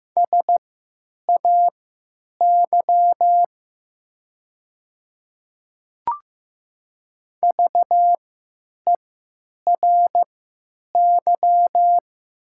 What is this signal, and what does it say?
Morse code, heard as a keyed tone, about 15 wpm, spaced as for 9 wpm, 700 Hz. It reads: SAY VERY
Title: 3x Character Spacing - Top 200 Words - 15wpm